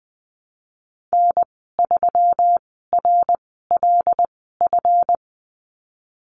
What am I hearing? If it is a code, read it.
Morse code, heard as a keyed tone, about 20 wpm, spaced as for 15 wpm, 700 Hz. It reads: N3RLF
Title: 147 Lesson 24g - Letter y - Call Signs 20wpm